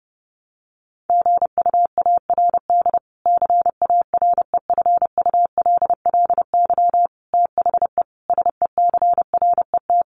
Morse code, heard as a keyed tone, about 30 wpm, 700 Hz. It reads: GUARD CAREFULLY THE SECRET
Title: Sentences from Top 1300 Words 30wpm